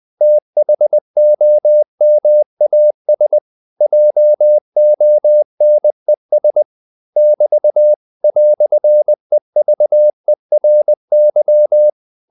Morse code, heard as a keyed tone, about 20 wpm, 600 Hz. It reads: THOMAS JONES <BT> "EVERY
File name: CW / Practice / EnglishQuotations0320